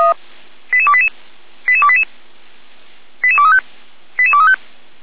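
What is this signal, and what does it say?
Five-tone selective call (ZVEI), heard as quick runs of five tones, tones of about 70 ms. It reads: 88288 88288 88235 88235
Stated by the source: ZVEI five-tone alert received by SDR, fire brigade feuerwehr-orlamuende.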